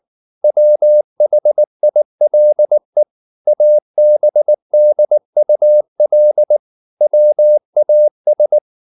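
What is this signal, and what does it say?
Morse code, heard as a keyed tone, about 19 wpm, 600 Hz. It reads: WHILE ABDUL WAS
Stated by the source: Morse code at 19 wpm from CW med Gnister, reading Ken Follett's novel Never in English.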